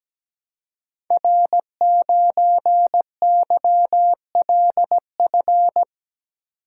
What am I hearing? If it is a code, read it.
Morse code, heard as a keyed tone, about 17 wpm, 700 Hz. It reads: R9YLF
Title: Code Groups - 5 Letter:Numbers - 17wpm